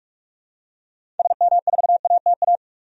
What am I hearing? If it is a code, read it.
Morse code, heard as a keyed tone, about 46 wpm, 700 Hz. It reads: SM4ATA
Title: DXCC-100 Call Signs - 6 Characters - 45wpm